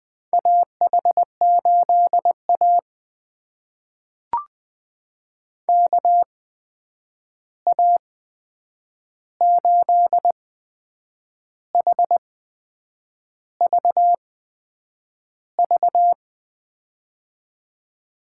Morse code, heard as a keyed tone, about 20 wpm, spaced as for 8 wpm, 700 Hz. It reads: AH8A KA8HVV
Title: USA Call Signs - 8x Character Spacing - 20wpm